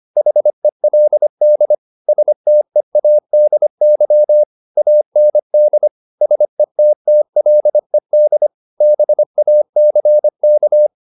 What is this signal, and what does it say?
Morse code, heard as a keyed tone, about 25 wpm, 600 Hz. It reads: HELD STEADY AND SETTLED BACK
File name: the old man and the sea part 1 25wpm